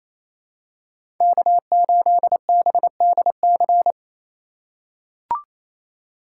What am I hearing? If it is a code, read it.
Morse code, heard as a keyed tone, about 27 wpm, 700 Hz. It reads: K8BDC
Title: USA Call Signs - 5 Characters - 2x Character Spacing - 28wpm